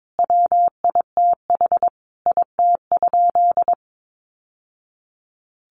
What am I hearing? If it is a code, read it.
Morse code, heard as a keyed tone, about 22 wpm, 700 Hz. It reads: WITH IT?
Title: Sentences from Top 1000 Words 22wpm